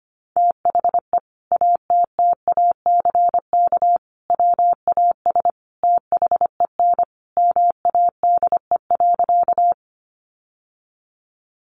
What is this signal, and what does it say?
Morse code, heard as a keyed tone, about 25 wpm, 700 Hz. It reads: THE ATTACK WAS THEN MADE.
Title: Sentences from Top 1500 Words 25wpm